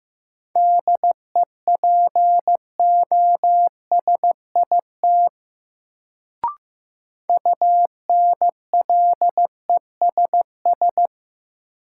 Morse code, heard as a keyed tone, about 15 wpm, 700 Hz. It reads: DEPOSIT UNLESS
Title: Sentences from Top 1300 Words - New Word Drill Alt - 15wpm